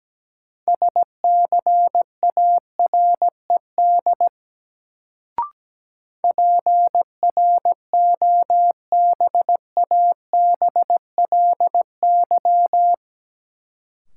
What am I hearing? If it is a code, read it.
Morse code, heard as a keyed tone, about 17 wpm, 700 Hz. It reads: SCARED PROBABLY
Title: Sentences from Top 700 Words - New Word Drill - 17wpm